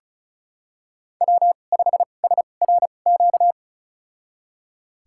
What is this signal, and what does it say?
Morse code, heard as a keyed tone, about 36 wpm, spaced as for 25 wpm, 700 Hz. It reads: W5SRQ
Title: USA Call Signs - 5 Characters - 2x Character Spacing - 35wpm